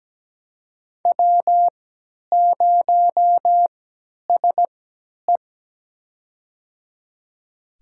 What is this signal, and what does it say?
Morse code, heard as a keyed tone, about 17 wpm, spaced as for 10 wpm, 700 Hz. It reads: W0SE
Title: USA Call Signs - 3x Character Spacing - 17wpm